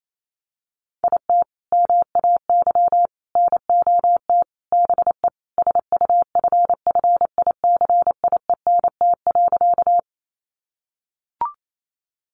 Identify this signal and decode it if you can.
Morse code, heard as a keyed tone, about 28 wpm, 700 Hz. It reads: IT MAY NOT BE SUFFICIENT.
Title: Sentences from Top 1400 Words 28wpm